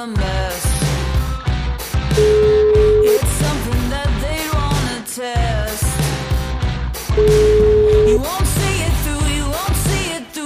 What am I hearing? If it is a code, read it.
Ringback tone (Europe, 425 Hz), with noise.